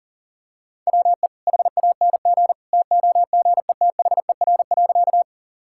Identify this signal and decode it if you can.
Morse code, heard as a keyed tone, about 40 wpm, 700 Hz. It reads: WE HANG TOGETHER.
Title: Sentences from Top 1600 Words 40wpm